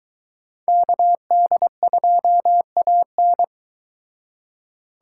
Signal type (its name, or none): Morse code